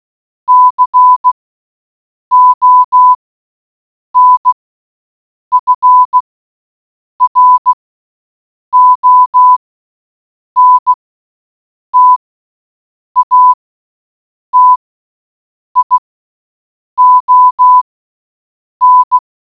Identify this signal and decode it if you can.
Morse code, heard as a keyed tone, about 16 wpm, spaced as for 7 wpm, 1000 Hz. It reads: CONFRONTATION